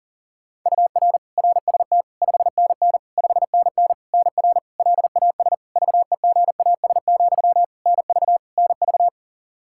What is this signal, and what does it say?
Morse code, heard as a keyed tone, about 40 wpm, 700 Hz. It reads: UR RST 5NN 5NN NR LAS VEGAS, NV NV
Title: K1USN SST QSO Practice 40wpm